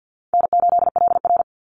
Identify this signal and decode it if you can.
Morse code, heard as a keyed tone, about 50 wpm, 700 Hz. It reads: N8LR
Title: Easy Call Signs - 50wpm